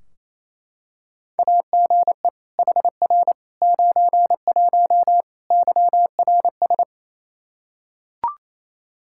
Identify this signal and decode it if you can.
Morse code, heard as a keyed tone, about 28 wpm, 700 Hz. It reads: AGE HR 91 YRS
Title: Practice QSO - Exchange 3 - More Info - 28wpm